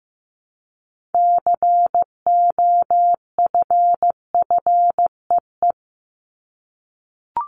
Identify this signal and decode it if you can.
Morse code, heard as a keyed tone, about 15 wpm, 700 Hz. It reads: COFFEE
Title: Top 1751-2000 Words 15wpm